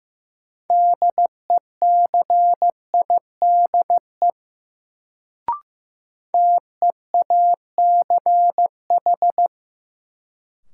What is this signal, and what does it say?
Morse code, heard as a keyed tone, about 15 wpm, 700 Hz. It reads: DECIDE TEACH
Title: Sentences from Top 500 Words - New Word Drill - 15wpm